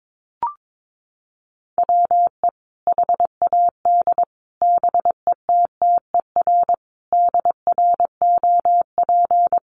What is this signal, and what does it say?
Morse code, heard as a keyed tone, about 22 wpm, 700 Hz. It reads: WE HAD BETTER DROP
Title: Sentences from Top 1400 Words 22wpm